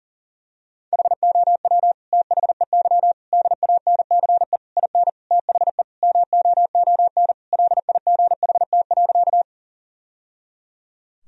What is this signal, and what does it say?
Morse code, heard as a keyed tone, about 40 wpm, 700 Hz. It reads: HOW THEY DANCE IN THE MOON LIGHT.
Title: Sentences from Top 1000 Words - Review Part 4 - 40wpm